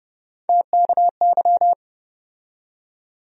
Morse code, heard as a keyed tone, about 30 wpm, 700 Hz. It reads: TKY
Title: USA Call Sign Postfixes - 3 Letters 30wpm